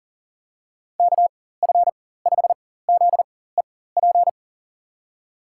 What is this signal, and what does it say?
Morse code, heard as a keyed tone, about 40 wpm, spaced as for 19 wpm, 700 Hz. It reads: KF5ZEP
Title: USA Call Signs - 6 Characters - 4x Character Spacing - 40wpm